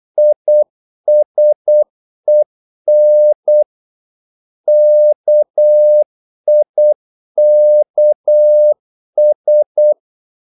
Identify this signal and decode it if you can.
Morse code, heard as a keyed tone, about 8 wpm, 600 Hz. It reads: ISEN KIKS
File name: Korte ord DK 8wpm